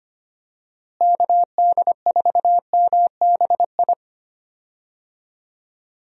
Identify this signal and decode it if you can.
Morse code, heard as a keyed tone, about 25 wpm, 700 Hz. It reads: KD4MBI
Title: USA Call Signs - 6 Characters - Repeated 2x - 25wpm